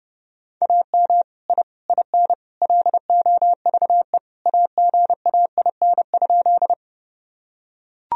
Morse code, heard as a keyed tone, about 32 wpm, 700 Hz. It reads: AM I IN LOVE AGAIN?
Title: Sentences from Top 1000 Words - Review Part 2 - 30wpm